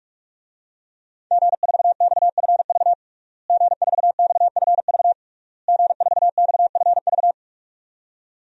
Morse code, heard as a keyed tone, about 46 wpm, 700 Hz. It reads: G4XFV G4XFV G4XFV
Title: DXCC-100 Call Signs - 5 Characters - Repeated 3x - 45wpm